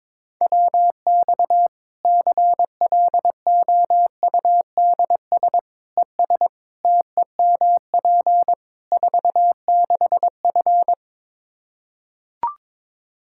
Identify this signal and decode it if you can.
Morse code, heard as a keyed tone, about 22 wpm, 700 Hz. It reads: WX CLOUDS ES TEMP 46F
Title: Practice QSO - Exchange 2 - Info - 22wpm